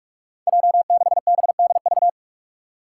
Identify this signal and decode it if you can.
Morse code, heard as a keyed tone, about 41 wpm, 700 Hz. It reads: J6BDV
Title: Five Challenging Letter-Number Combinations 45wpm